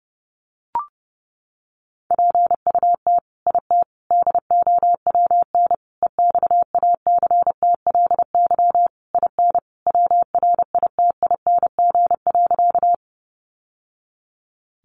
Morse code, heard as a keyed tone, about 30 wpm, 700 Hz. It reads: PUT IT DOWN EXACTLY IN WRITING.
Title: Sentences from Top 900 Words 30wpm